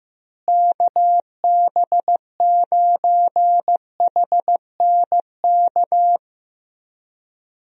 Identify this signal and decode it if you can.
Morse code, heard as a keyed tone, about 16 wpm, 700 Hz. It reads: KB9HNK
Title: USA Call Signs - 6 Characters - Repeated 3x - 15wpm